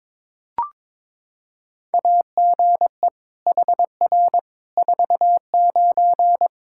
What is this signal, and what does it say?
Morse code, heard as a keyed tone, about 22 wpm, 700 Hz. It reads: AGE HR 49
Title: Practice QSO - Exchange 3 - More Info - 22wpm